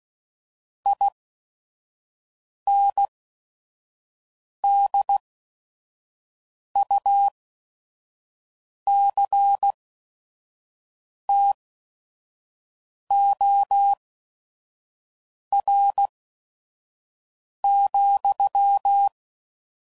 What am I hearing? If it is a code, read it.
Morse code, heard as a keyed tone, about 16 wpm, spaced as for 5 wpm, 800 Hz. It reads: INDUCTOR,
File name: CW Code Groups @5WPM / Technical Subjects / Reactance